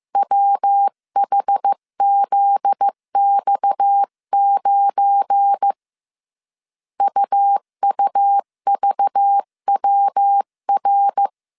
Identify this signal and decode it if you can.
Morse code, heard as a keyed tone, about 15 wpm, 800 Hz. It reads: WHZX9 UUVWR
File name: CW / DET / 1 9 1500C S B 14wpm 3302